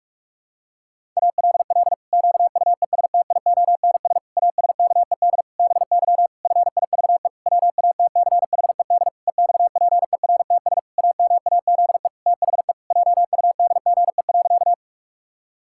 Morse code, heard as a keyed tone, about 41 wpm, 700 Hz. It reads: ALL QUESTIONS ASKED BY FIVE WATCHED EXPERTS AMAZE THE JUDGE.
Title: Pangram Challenge v2 - 45wpm